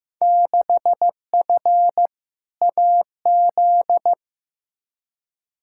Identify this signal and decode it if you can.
Morse code, heard as a keyed tone, about 15 wpm, 700 Hz. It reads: 6F AZ
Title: Field Day Exchanges Encore - 15wpm